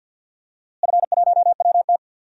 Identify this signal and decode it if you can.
Morse code, heard as a keyed tone, about 51 wpm, 700 Hz. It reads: F1WT
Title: DXCC-100 Call Signs - 4 Characters - Repeated 3x - 50wpm